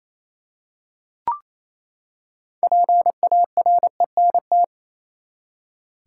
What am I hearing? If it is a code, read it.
Morse code, heard as a keyed tone, about 28 wpm, 700 Hz. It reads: PARENT
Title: Six Letter Words - 28wpm